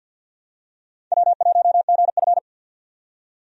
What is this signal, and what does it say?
Morse code, heard as a keyed tone, about 47 wpm, 700 Hz. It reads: W1GF
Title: USA Call Signs - 4 Characters - 50wpm